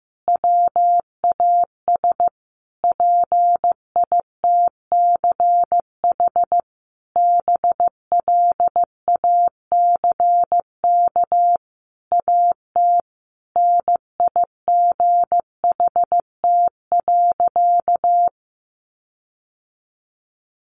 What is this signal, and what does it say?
Morse code, heard as a keyed tone, about 15 wpm, 700 Hz. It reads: WAS PITCH BLACK AT NIGHT.
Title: Sentences from Top 1300 Words 15wpm